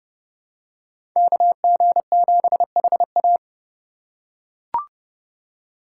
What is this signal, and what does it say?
Morse code, heard as a keyed tone, about 32 wpm, 700 Hz. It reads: KG7HA
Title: USA Call Signs - 5 Characters - 3x Character Spacing - 30wpm